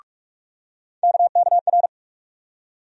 Morse code, heard as a keyed tone, about 44 wpm, 700 Hz. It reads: KKR
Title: USA Call Sign Postfixes - 3 Letters 45wpm